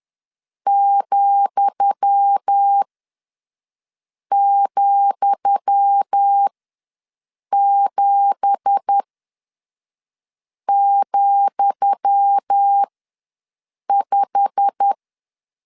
Morse code, heard as a keyed tone, about 10 wpm, spaced as for 5 wpm, 800 Hz. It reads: ,,7,5
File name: LEZIONI / 10 SIGNS / 10D COMA ws 10wpm 4314